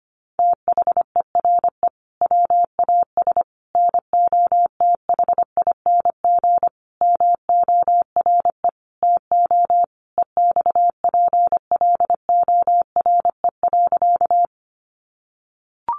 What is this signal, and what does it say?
Morse code, heard as a keyed tone, about 25 wpm, 700 Hz. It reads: THERE WAS NOTHING MORE TO EXPLORE.
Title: Sentences from Top 1100 Words 25wpm